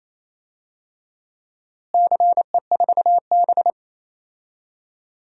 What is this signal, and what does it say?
Morse code, heard as a keyed tone, about 27 wpm, 700 Hz. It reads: CE4B
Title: DXCC-100 Call Signs - 4 Characters - Repeated 2x - 28wpm